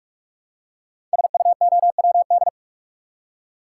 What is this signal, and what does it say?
Morse code, heard as a keyed tone, about 44 wpm, 700 Hz. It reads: SUOWD
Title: Code Groups - 5 Letters - 45wpm